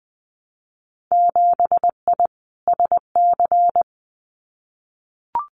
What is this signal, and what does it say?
Morse code, heard as a keyed tone, about 20 wpm, 700 Hz. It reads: 7I SC
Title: Winter Field Day Exchanges Encore 20wpm